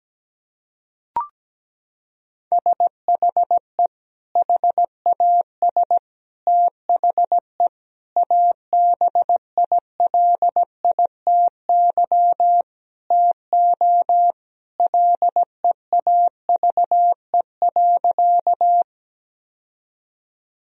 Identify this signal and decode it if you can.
Morse code, heard as a keyed tone, about 17 wpm, 700 Hz. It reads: SHE HAS THE ABILITY TO LEAVE.
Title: Sentences from Top 1000 Words - Review Part 3 - 17wpm